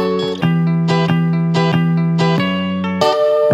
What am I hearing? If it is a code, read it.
There is no code.